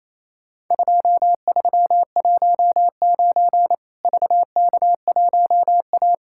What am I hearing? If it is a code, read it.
Morse code, heard as a keyed tone, about 27 wpm, 700 Hz. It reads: 2319 VK1A
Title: Worked All Europe DX Contest - QTC Exchange - 28wpm